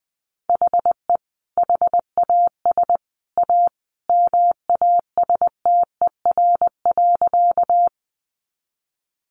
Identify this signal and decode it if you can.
Morse code, heard as a keyed tone, about 20 wpm, 700 Hz. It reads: HE HAS A MASTER.
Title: Sentences from Top 1300 Words 20wpm